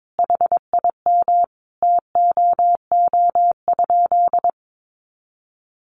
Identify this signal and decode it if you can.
Morse code, heard as a keyed tone, about 22 wpm, 700 Hz. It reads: HIM TOO?
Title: Sentences from Top 500 Words 22wpm